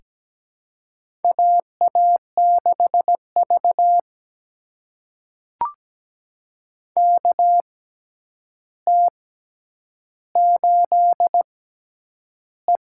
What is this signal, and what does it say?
Morse code, heard as a keyed tone, about 17 wpm, spaced as for 9 wpm, 700 Hz. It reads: AA6V KT8E
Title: USA Call Signs - 4 Characters - 6x Character Spacing - 17wpm